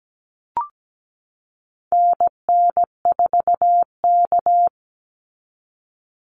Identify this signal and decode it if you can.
Morse code, heard as a keyed tone, about 17 wpm, 700 Hz. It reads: NN4K
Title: Call Signs Repeated 3x - 17wpm